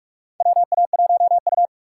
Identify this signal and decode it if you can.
Morse code, heard as a keyed tone, about 46 wpm, 700 Hz. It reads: WA1U
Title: USA Call Signs - 4 Characters - 4x Character Spacing - 45wpm